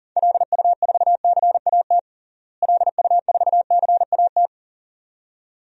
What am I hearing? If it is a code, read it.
Morse code, heard as a keyed tone, about 40 wpm, 700 Hz. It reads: LU4CAT LU4CAT
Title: DXCC-100 Call Signs - 6 Characters - Repeated 2x - 40wpm